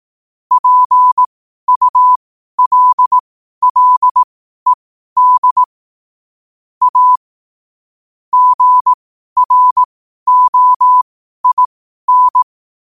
Morse code, heard as a keyed tone, about 18 wpm, spaced as for 12 wpm, 1000 Hz. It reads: PULLED A GROIN
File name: qotdfiles / 12wpm